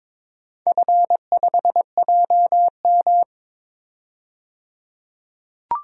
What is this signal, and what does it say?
Morse code, heard as a keyed tone, about 22 wpm, 700 Hz. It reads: F5JM
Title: DXCC-100 Call Signs - 4 Characters - 22wpm